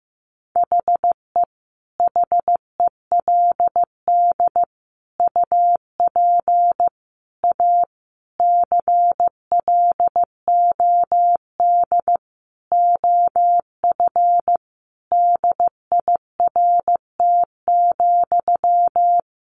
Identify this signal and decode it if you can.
Morse code, heard as a keyed tone, about 15 wpm, 700 Hz. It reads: HE HELD UP A CLOD OF DIRT,
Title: A Sound Of Thunder 15wpm